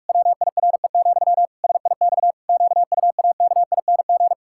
Morse code, heard as a keyed tone, about 46 wpm, 700 Hz. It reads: WIRE, SIX QUAKING
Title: Pangram Challenge v2 - 45wpm